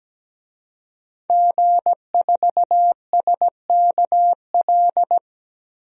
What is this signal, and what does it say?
Morse code, heard as a keyed tone, about 17 wpm, 700 Hz. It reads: G4SKL
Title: DXCC-100 Call Signs - 5 Characters - Repeated 3x - 17wpm